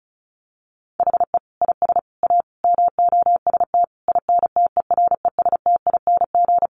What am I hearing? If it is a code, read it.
Morse code, heard as a keyed tone, about 34 wpm, 700 Hz. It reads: HE IS A MOST INTERESTING
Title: Sentences from Top 1000 Words 35wpm